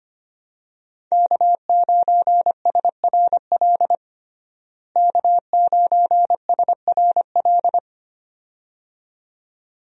Morse code, heard as a keyed tone, about 25 wpm, 700 Hz. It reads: K9SRL K9SRL
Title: USA Call Signs - 5 Characters - Repeated 2x - 25wpm